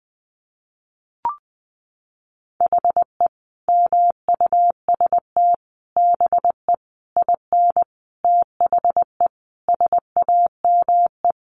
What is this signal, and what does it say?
Morse code, heard as a keyed tone, about 20 wpm, 700 Hz. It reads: HE MUST BE IN THE SAME
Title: Sentences from Top 500 Words 20wpm